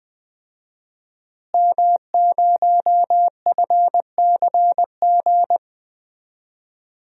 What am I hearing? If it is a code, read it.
Morse code, heard as a keyed tone, about 20 wpm, 700 Hz. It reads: M0FCG
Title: DXCC-100 Call Signs - 5 Characters - Repeated 2x - 20wpm